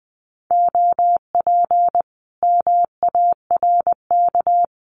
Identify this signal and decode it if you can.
Morse code, heard as a keyed tone, about 20 wpm, 700 Hz. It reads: OP MARK
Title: Common QSO Phrases Encore 20wpm